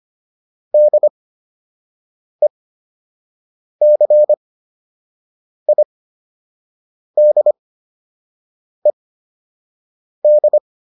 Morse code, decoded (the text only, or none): DECIDED